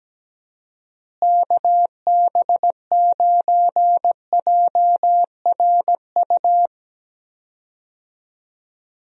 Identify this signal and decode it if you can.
Morse code, heard as a keyed tone, about 17 wpm, 700 Hz. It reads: KB9JRU
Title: USA Call Signs - 6 Characters - Repeated 2x - 17wpm